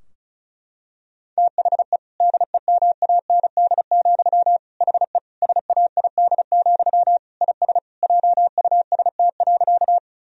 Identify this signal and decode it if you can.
Morse code, heard as a keyed tone, about 36 wpm, 700 Hz. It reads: THE DEMAND, HE SAID, IS JUST.
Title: Sentences from Top 1000 Words - Review Part 4 - 35wpm